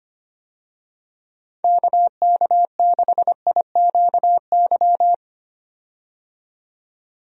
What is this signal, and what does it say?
Morse code, heard as a keyed tone, about 25 wpm, 700 Hz. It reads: KK6IQY
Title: USA Call Signs - 6 Characters - 25wpm